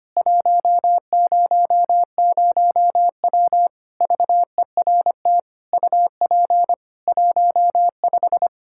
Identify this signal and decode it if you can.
Morse code, heard as a keyed tone, about 25 wpm, 700 Hz. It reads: 100W VERT UP 15
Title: K1USN SST QSO Practice 25wpm